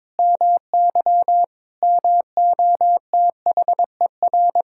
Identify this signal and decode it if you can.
Morse code, heard as a keyed tone, about 22 wpm, 700 Hz. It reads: MY MOTHER
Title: Sentences from Top 1000 Words - Review Part 6 - 22wpm